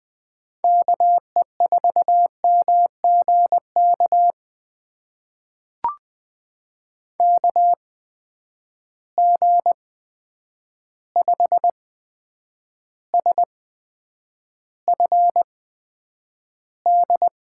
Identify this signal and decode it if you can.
Morse code, heard as a keyed tone, about 20 wpm, spaced as for 9 wpm, 700 Hz. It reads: KE4MGK KG5SFD
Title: USA Call Signs - 6 Characters - 8x Character Spacing - 20wpm